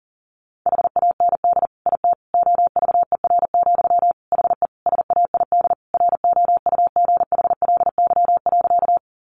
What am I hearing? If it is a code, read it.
Morse code, heard as a keyed tone, about 40 wpm, 700 Hz. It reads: HAND IT OVER, HE SAID ROUGHLY.
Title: Sentences from Top 1500 Words 40wpm